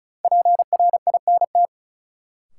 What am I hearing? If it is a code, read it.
Morse code, heard as a keyed tone, about 36 wpm, 700 Hz. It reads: PRINT
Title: Sentences from Top 1400 Words - New Word Drill Alt - 35wpm